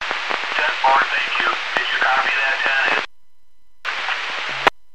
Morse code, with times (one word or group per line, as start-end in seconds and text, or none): none